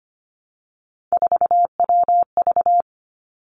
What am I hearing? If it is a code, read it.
Morse code, heard as a keyed tone, about 25 wpm, 700 Hz. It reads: 4WV
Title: Three Challenging Letter-Number Combinations 25wpm